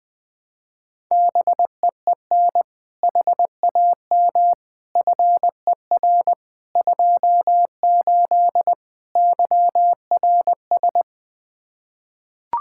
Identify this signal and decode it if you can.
Morse code, heard as a keyed tone, about 20 wpm, 700 Hz. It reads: BEEN HAM FER 28 YRS
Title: K1USN SST QSO Practice 20wpm